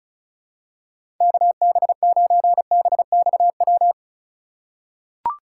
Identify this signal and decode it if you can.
Morse code, heard as a keyed tone, about 34 wpm, 700 Hz. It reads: KB9BXW
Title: USA Call Signs - 6 Characters - 3x Character Spacing - 35wpm